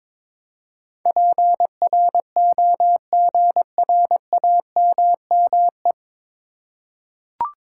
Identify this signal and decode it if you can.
Morse code, heard as a keyed tone, about 22 wpm, 700 Hz. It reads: PROGRAMME
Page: Morse Code Ninja practice 22wpm